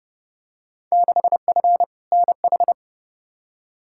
Morse code, heard as a keyed tone, about 30 wpm, 700 Hz. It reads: 6F NH
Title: Field Day Exchanges Encore - 30wpm